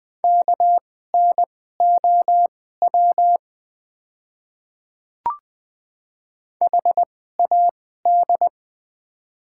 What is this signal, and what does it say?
Morse code, heard as a keyed tone, about 20 wpm, spaced as for 14 wpm, 700 Hz. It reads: KNOW HAD
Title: 2x Character Spacing - Top 100 Words - 20wpm